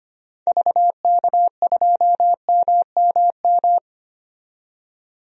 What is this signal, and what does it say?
Morse code, heard as a keyed tone, about 25 wpm, 700 Hz. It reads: VK2MMM
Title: DXCC-100 Call Signs - 6 Characters - 25wpm